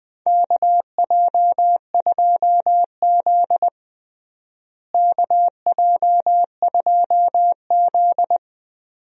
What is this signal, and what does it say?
Morse code, heard as a keyed tone, about 20 wpm, 700 Hz. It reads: KJ2Z KJ2Z
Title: USA Call Signs - 4 Characters - Repeated 2x - 20wpm